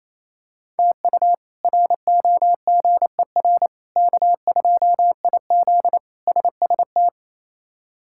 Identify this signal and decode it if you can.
Morse code, heard as a keyed tone, about 28 wpm, 700 Hz. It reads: TU ROGER K2IZ SST
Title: K1USN SST QSO Practice - 28wpm